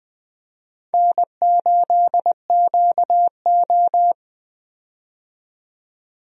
Morse code, heard as a keyed tone, about 20 wpm, 700 Hz. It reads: N8QO
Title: USA Call Signs - 4 Characters - 20wpm